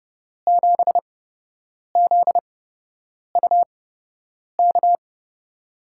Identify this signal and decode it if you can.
Morse code, heard as a keyed tone, about 30 wpm, spaced as for 8 wpm, 700 Hz. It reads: 7ZUK
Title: USA Call Signs - 6 Characters - 8x Character Spacing - 30wpm